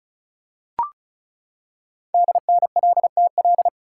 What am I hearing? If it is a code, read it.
Morse code, heard as a keyed tone, about 36 wpm, 700 Hz. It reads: DNLTL
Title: Code Groups - 5 Letters - 35wpm